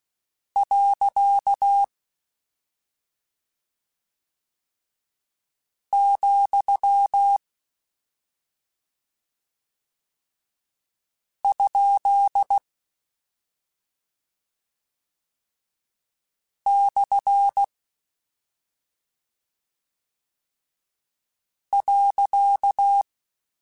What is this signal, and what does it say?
Morse code, heard as a keyed tone, about 16 wpm, 800 Hz. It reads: . , ? / .
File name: Misc / Misc-mp3s / Punct @5